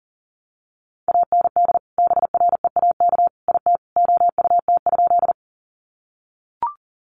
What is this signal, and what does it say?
Morse code, heard as a keyed tone, about 40 wpm, 700 Hz. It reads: AND BREAK IT OUT?
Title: Sentences from Top 900 Words 40wpm